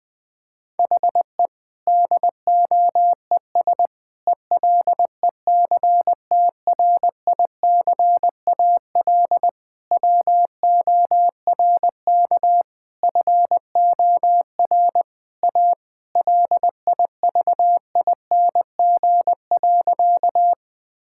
Morse code, heard as a keyed tone, about 20 wpm, 700 Hz. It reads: HE DOES ELECTRICAL WORK FOR A LIVING.
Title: Sentences from Top 1000 Words - Review Part 6 - 20wpm